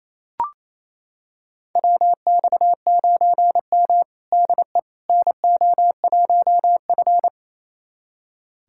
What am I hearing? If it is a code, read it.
Morse code, heard as a keyed tone, about 28 wpm, 700 Hz. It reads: WX9M DE NO1F
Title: USA Callsign DE Callsign - 4 Characters - 28wpm